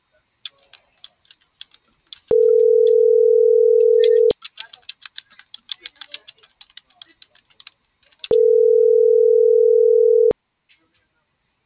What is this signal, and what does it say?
Ringback tone (North America), 440 + 480 Hz.